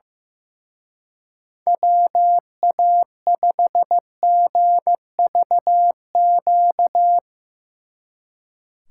Morse code, heard as a keyed tone, about 16 wpm, 700 Hz. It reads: WA5GVQ
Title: Speed Racing - USA 2x3 Call Signs 15wpm